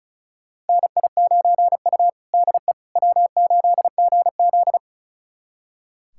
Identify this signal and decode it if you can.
Morse code, heard as a keyed tone, about 34 wpm, 700 Hz. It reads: NI9U DE W8GZ
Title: USA Callsign DE Callsign - 4 Characters - 35wpm